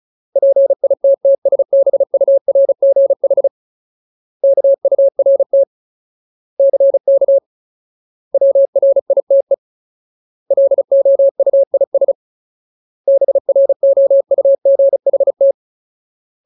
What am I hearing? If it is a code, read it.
Morse code, heard as a keyed tone, about 34 wpm, 550 Hz. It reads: PITTSBURGH KURT CK WRITE LOUIS BROUGHT
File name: Ultimate Word List / 35WPM Ultimate